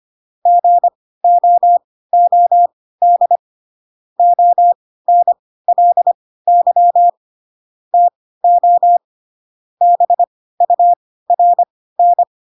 Morse code, heard as a keyed tone, about 25 wpm, spaced as for 16 wpm, 700 Hz. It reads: GOOD ONLY TO BURN
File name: Pinocchio / Pinocchio ch1 10 25 ik1hge